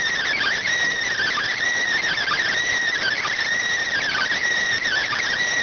SSTV (slow-scan television), PD 290.